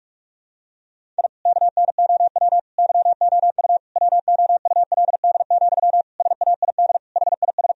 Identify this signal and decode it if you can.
Morse code, heard as a keyed tone, about 46 wpm, 700 Hz. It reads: I KNOW YOU WOULD, SAID HIS
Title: Sentences from Top 1000 Words - Review Part 1 - 45wpm